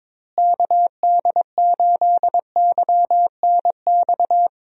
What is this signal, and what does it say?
Morse code, heard as a keyed tone, about 22 wpm, 700 Hz. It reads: KD8YNX
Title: USA Call Signs - 6 Characters - Repeated 2x - 22wpm